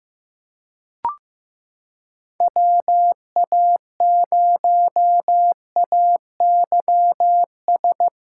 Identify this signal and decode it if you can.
Morse code, heard as a keyed tone, about 15 wpm, 700 Hz. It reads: WA0AYS
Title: USA Callsign DE Callsign - 6 Characters - 15wpm